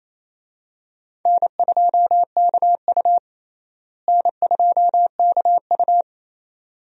Morse code, heard as a keyed tone, about 28 wpm, 700 Hz. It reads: N2KU N2KU
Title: USA Call Signs - 4 Characters - Repeated 2x - 28wpm